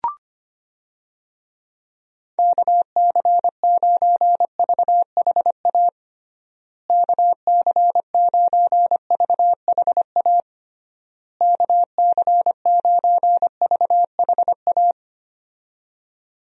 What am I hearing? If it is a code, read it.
Morse code, heard as a keyed tone, about 25 wpm, 700 Hz. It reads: KC9VHA KC9VHA KC9VHA
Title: USA Call Signs - 6 Characters - Repeated 3x - 25wpm